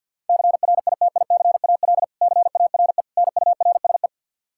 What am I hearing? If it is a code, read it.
Morse code, heard as a keyed tone, about 50 wpm, 700 Hz. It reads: CRITICAL CARE NURSE
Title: Practice QSO - Exchange 3 - More Info - 50wpm